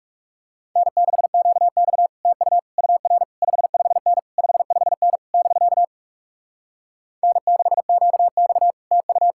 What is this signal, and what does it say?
Morse code, heard as a keyed tone, about 40 wpm, 700 Hz. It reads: N6QX TU UR 55N 55N <BK> N6QX TU